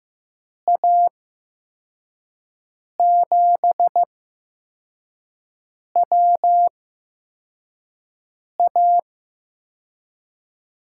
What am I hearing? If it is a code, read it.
Morse code, heard as a keyed tone, about 15 wpm, spaced as for 4 wpm, 700 Hz. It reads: A7WA
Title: USA Call Signs - 5 Characters - 8x Character Spacing - 15wpm